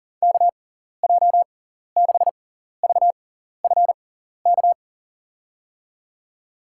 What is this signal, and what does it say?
Morse code, heard as a keyed tone, about 40 wpm, spaced as for 14 wpm, 700 Hz. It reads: KJ6VFK
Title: USA Call Signs - 6 Characters - 6x Character Spacing - 40wpm